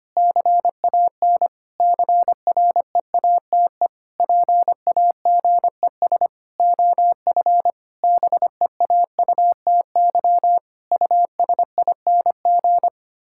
Morse code, heard as a keyed tone, about 25 wpm, 700 Hz. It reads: CAN CREATE PAGES OF BEAUTY USING
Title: Pangram Challenge v2 - 25wpm